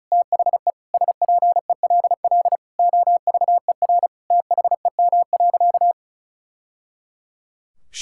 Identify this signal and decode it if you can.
Morse code, heard as a keyed tone, about 34 wpm, 700 Hz. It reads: THE SPELL OVER THEM.
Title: Sentences from Top 1000 Words - Review Part 6 - 35wpm